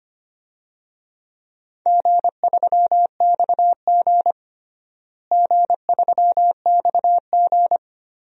Morse code, heard as a keyed tone, about 25 wpm, 700 Hz. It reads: G3XG G3XG
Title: DXCC-100 Call Signs - 4 Characters - Repeated 3x - 25wpm